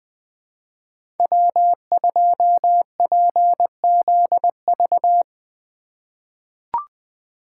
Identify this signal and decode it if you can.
Morse code, heard as a keyed tone, about 20 wpm, 700 Hz. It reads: W2PZV
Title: USA Call Signs - 5 Characters - 4x Character Spacing - 20wpm